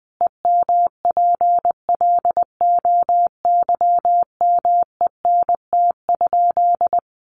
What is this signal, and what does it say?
Morse code, heard as a keyed tone, about 20 wpm, 700 Hz. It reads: EMPLOYMENT?
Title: Sentences from Top 1500 Words 20wpm